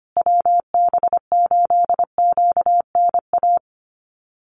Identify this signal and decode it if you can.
Morse code, heard as a keyed tone, about 25 wpm, 700 Hz. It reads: WB8QNA
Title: USA Call Signs Repeated 3x 25wpm